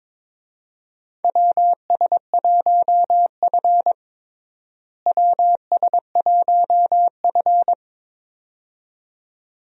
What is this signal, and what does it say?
Morse code, heard as a keyed tone, about 21 wpm, 700 Hz. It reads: WS1F WS1F
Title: USA Call Signs - 4 Characters - Repeated 2x - 22wpm